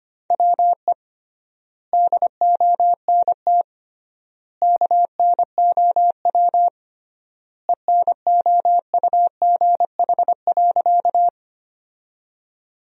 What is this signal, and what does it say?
Morse code, heard as a keyed tone, about 25 wpm, 700 Hz. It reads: WE DONT KNOW ENOUGH.
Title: 3x Word-Spacing - Sentences from Top 400 Words - 25wpm